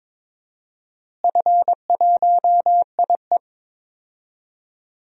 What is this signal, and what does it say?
Morse code, heard as a keyed tone, about 22 wpm, 700 Hz. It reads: F1IE
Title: DXCC-100 Call Signs - 4 Characters - Repeated 3x - 22wpm